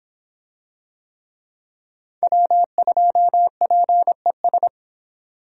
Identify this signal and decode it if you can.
Morse code, heard as a keyed tone, about 25 wpm, 700 Hz. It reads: W2PES